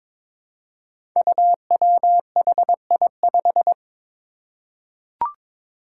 Morse code, heard as a keyed tone, about 22 wpm, 700 Hz. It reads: UWHI5
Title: Five Challenging Letter-Number Combinations 22wpm